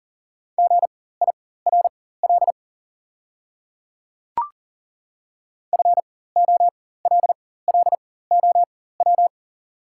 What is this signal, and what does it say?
Morse code, heard as a keyed tone, about 37 wpm, spaced as for 18 wpm, 690 Hz. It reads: GIRL FOLLOW
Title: 4x Character Spacing - Top 500 Words - 40wpm